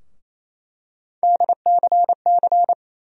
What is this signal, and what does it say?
Morse code, heard as a keyed tone, about 28 wpm, 700 Hz. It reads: DCC
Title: USA Call Sign Postfixes - 3 Letters 28wpm